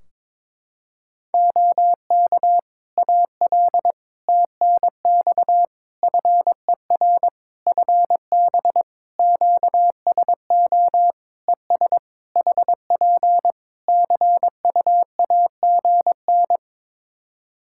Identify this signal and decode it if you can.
Morse code, heard as a keyed tone, about 23 wpm, 700 Hz. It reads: OK AL TNX FER FB QSO ES HP CUAGN
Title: Practice QSO - Exchange 4 - Ending - 22wpm